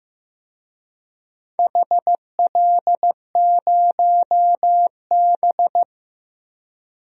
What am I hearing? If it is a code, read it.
Morse code, heard as a keyed tone, about 15 wpm, 700 Hz. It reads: HL0B
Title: DXCC-100 Call Signs - 4 Characters - Repeated 2x - 15wpm